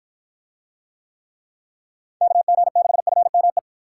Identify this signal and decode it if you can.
Morse code, heard as a keyed tone, about 53 wpm, 700 Hz. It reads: KG6FME